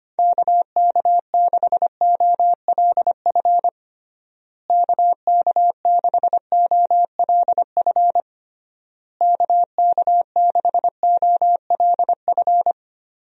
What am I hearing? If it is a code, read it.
Morse code, heard as a keyed tone, about 25 wpm, 700 Hz. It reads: KK6OLF KK6OLF KK6OLF
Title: USA Call Signs - 6 Characters - Repeated 3x - 25wpm